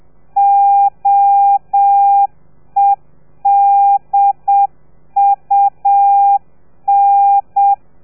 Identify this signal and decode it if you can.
Morse code, heard as a keyed tone, about 7 wpm, 800 Hz. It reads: OEDUN